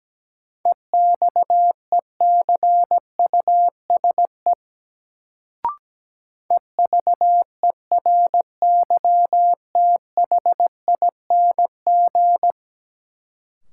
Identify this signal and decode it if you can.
Morse code, heard as a keyed tone, about 17 wpm, 700 Hz. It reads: EXECUSE EVERYTHING
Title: Sentences from Top 700 Words - New Word Drill - 17wpm